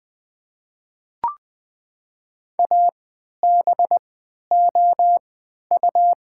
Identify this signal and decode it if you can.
Morse code, heard as a keyed tone, about 20 wpm, spaced as for 11 wpm, 700 Hz. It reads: ABOU
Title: 3x Character Spacing - Top 100 Words - 20wpm